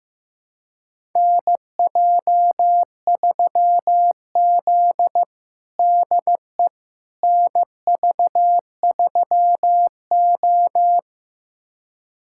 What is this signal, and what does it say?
Morse code, heard as a keyed tone, about 15 wpm, 700 Hz. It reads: NJ3Z DE NV3O
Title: USA Callsign DE Callsign - 4 Characters - 15wpm